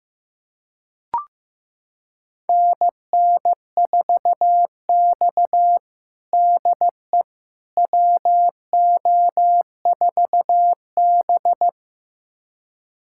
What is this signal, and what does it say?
Morse code, heard as a keyed tone, about 16 wpm, 700 Hz. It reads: NN4X DE WO4B
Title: USA Callsign DE Callsign - 4 Characters - 15wpm